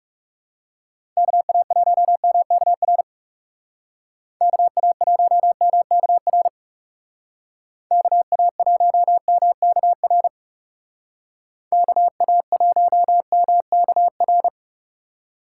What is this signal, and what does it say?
Morse code, heard as a keyed tone, about 39 wpm, 700 Hz. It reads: KA1MKR KA1MKR KA1MKR KA1MKR